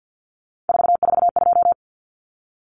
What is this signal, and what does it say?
Morse code, heard as a keyed tone, about 50 wpm, 700 Hz. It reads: 442
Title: Numbers - 3-Digit 50wpm